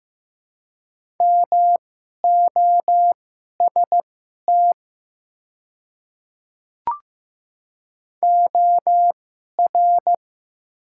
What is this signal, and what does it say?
Morse code, heard as a keyed tone, about 15 wpm, spaced as for 11 wpm, 700 Hz. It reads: MOST OR
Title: 2x Character Spacing - Top 100 Words - 15wpm